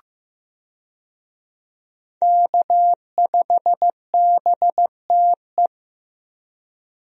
Morse code, heard as a keyed tone, about 15 wpm, 700 Hz. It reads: K5BTE
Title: USA Call Signs - 5 Characters - Repeated 3x - 15wpm